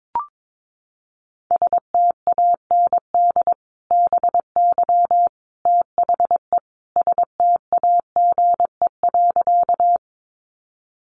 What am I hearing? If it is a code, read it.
Morse code, heard as a keyed tone, about 22 wpm, 700 Hz. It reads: STAND BY THE STAGE.
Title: Sentences from Top 1200 Words 22wpm